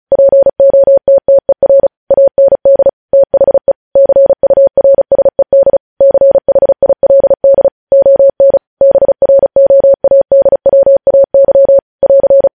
Morse code, heard as a keyed tone, about 36 wpm, 550 Hz. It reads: POTTER AND THE CURSED CHILD ON BROADWAY <AR>
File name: podcast / News headlines-1x-35x35